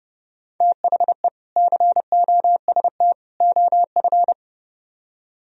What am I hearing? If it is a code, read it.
Morse code, heard as a keyed tone, about 36 wpm, 700 Hz. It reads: THE COST OF
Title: Speed-Racing - Sets of 3 Words 30wpm